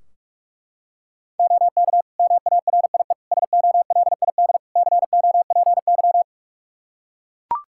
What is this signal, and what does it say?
Morse code, heard as a keyed tone, about 42 wpm, 700 Hz. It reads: OK MARIE SOLID COPY
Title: Practice QSO - Exchange 3 - More Info - 45wpm